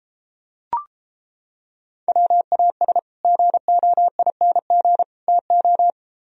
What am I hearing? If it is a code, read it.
Morse code, heard as a keyed tone, about 29 wpm, 700 Hz. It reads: WAS GOING TO